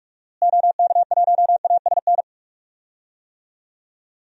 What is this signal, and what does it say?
Morse code, heard as a keyed tone, about 42 wpm, 700 Hz. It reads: OK1ASN
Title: DXCC-100 Call Signs - 6 Characters - Repeated 3x - 45wpm